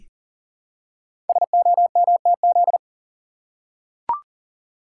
Morse code, heard as a keyed tone, about 39 wpm, 700 Hz. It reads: SOMTZ
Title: Code Groups - 5 Letter:Numbers - 40wpm